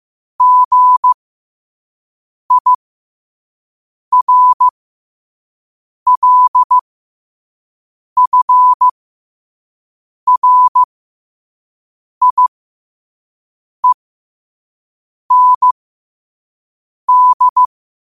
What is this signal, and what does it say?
Morse code, heard as a keyed tone, about 15 wpm, spaced as for 5 wpm, 1000 Hz. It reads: GIRLFRIEND
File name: 5wpm